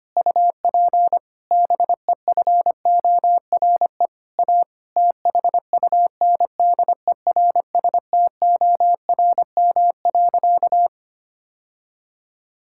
Morse code, heard as a keyed tone, about 25 wpm, 700 Hz. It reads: UP BEFORE A THUNDERSTORM.
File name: practice / 25wpm / Difficult Sentences with Hints - 25wpm